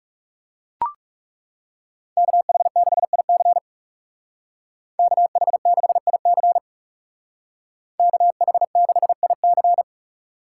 Morse code, heard as a keyed tone, about 37 wpm, 700 Hz. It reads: KH6IC KH6IC KH6IC
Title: Speed Racing - USA 2x2 Call Signs 30wpm